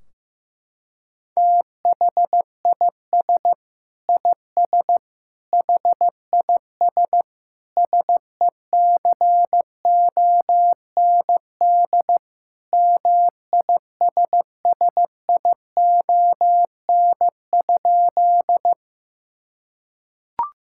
Morse code, heard as a keyed tone, about 15 wpm, 700 Hz. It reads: THIS IS HIS SECOND MISSION?
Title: Sentences from Top 1600 Words 15wpm